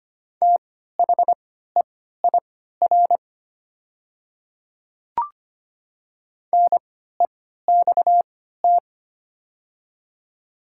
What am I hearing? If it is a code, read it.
Morse code, heard as a keyed tone, about 25 wpm, spaced as for 14 wpm, 700 Hz. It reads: THEIR NEXT
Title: 3x Character Spacing - Top 300 Words - 25wpm